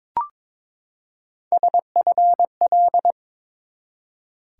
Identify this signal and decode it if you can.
Morse code, heard as a keyed tone, about 22 wpm, 700 Hz. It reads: SFL
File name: ARRL Sections v2 - 22wpm